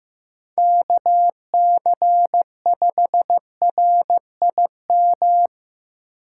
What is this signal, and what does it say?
Morse code, heard as a keyed tone, about 15 wpm, 700 Hz. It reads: KC5RIM
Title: USA Call Signs - 6 Characters - 3x Character Spacing - 15wpm